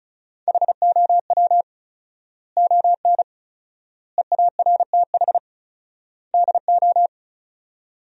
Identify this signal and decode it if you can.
Morse code, heard as a keyed tone, about 34 wpm, 700 Hz. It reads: HOW ON EARTH DO
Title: 4x Word-Spacing - Sentences from Top 200 Words - 35wpm